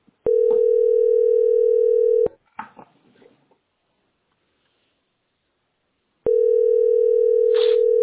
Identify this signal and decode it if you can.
Ringback tone (North America), 440 + 480 Hz.